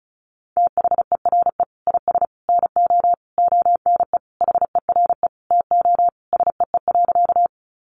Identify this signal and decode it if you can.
Morse code, heard as a keyed tone, about 34 wpm, 700 Hz. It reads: THERE IS NO ONE HERE TO SEE.
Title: Sentences from Top 200 Words 35wpm